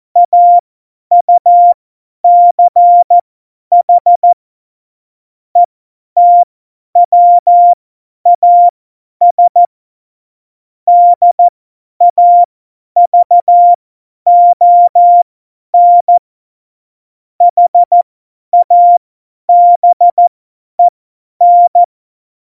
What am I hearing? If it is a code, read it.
Morse code, heard as a keyed tone, about 14 wpm, spaced as for 10 wpm, 700 Hz. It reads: AUCH ETWAS DAVON HABEN
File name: Morsekurs / Witze / Witz 28. Mai 2019, 50 F70